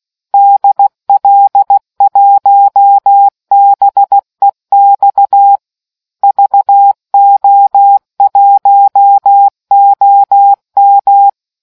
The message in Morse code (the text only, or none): DL1BEX VO1OM